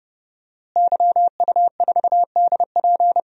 Morse code, heard as a keyed tone, about 30 wpm, 700 Hz. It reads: YU4DP
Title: DXCC-100 Call Signs - 5 Characters - Repeated 2x - 30wpm